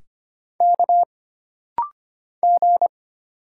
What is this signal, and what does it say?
Morse code, heard as a keyed tone, about 25 wpm, 700 Hz. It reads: K G
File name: Single Character (Letter:Number:Punctuation:Prosign) - Rapid-Fire - 25wpm